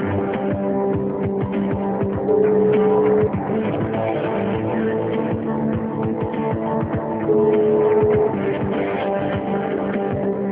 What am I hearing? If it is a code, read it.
Ringback tone (Europe, 425 Hz), with noise.
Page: Guarapuava - PR